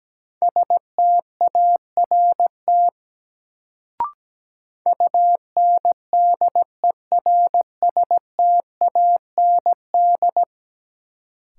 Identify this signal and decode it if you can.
Morse code, heard as a keyed tone, about 17 wpm, 700 Hz. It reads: START UNDERSTAND
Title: Sets of 4 Words - New Word Drill - 17wpm